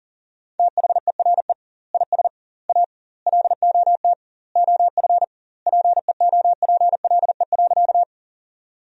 Morse code, heard as a keyed tone, about 40 wpm, 700 Hz. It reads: THERE IS A LOT OF PEOPLE.
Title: 2x Word-Spacing - Sentences from Top 500 Words - 40wpm